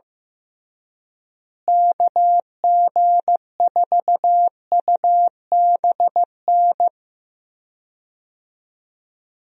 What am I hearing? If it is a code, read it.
Morse code, heard as a keyed tone, about 15 wpm, 700 Hz. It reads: KG4UBN
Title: USA Call Signs - 6 Characters - Repeated 3x - 15wpm